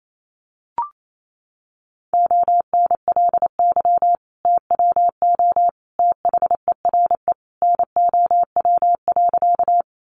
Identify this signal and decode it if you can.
Morse code, heard as a keyed tone, about 28 wpm, 700 Hz. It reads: ONLY TWO THERE NOW.
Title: Sentences from Top 200 Words 28wpm